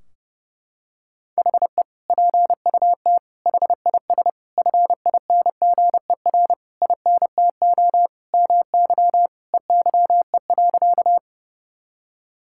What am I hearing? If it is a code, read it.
Morse code, heard as a keyed tone, about 32 wpm, 700 Hz. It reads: HE PUT HIS FINGER INTO MY EYE.
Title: Sentences from Top 1600 Words 30wpm